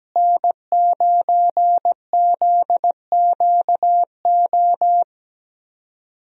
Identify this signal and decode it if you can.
Morse code, heard as a keyed tone, about 17 wpm, 700 Hz. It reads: N9ZQO
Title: USA Call Signs - 5 Characters - Repeated 2x - 17wpm